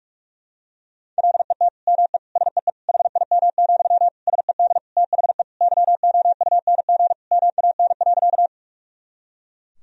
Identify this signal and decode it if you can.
Morse code, heard as a keyed tone, about 46 wpm, 700 Hz. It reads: LET ME SEE HIM, SED THE YOUNG MAN.
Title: Sentences from Top 400 Words and Abbreviations 45wpm